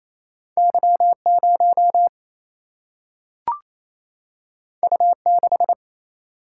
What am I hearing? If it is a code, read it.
Morse code, heard as a keyed tone, about 28 wpm, 700 Hz. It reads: Y 0 U 6
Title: Single-Letter Single-Number 28wpm